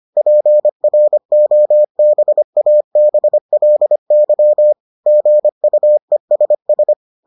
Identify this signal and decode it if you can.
Morse code, heard as a keyed tone, about 25 wpm, 600 Hz. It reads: PROBABLY GUESS